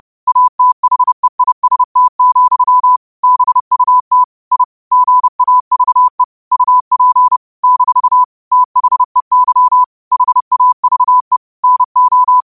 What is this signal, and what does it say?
Morse code, heard as a keyed tone, about 30 wpm, 1000 Hz. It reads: ATHEIST, BUT I GAVE UP - THEY HAVE NO
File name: qotdfiles / 30wpm